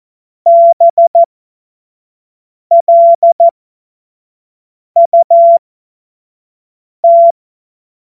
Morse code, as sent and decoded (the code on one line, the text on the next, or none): -... .-.. ..- -
BLUT